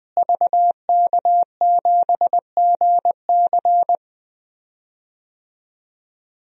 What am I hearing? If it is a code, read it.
Morse code, heard as a keyed tone, about 20 wpm, 700 Hz. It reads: VK7GC
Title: DXCC-100 Call Signs - 5 Characters - Repeated 3x - 20wpm